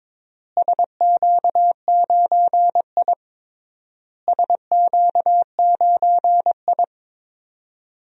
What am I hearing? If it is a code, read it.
Morse code, heard as a keyed tone, about 22 wpm, 700 Hz. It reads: SQ9I SQ9I
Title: DXCC-100 Call Signs - 4 Characters - Repeated 3x - 22wpm